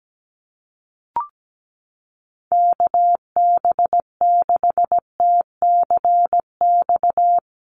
Morse code, heard as a keyed tone, about 17 wpm, 700 Hz. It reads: KB6TCX
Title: USA Call Signs Repeated 2x 17wpm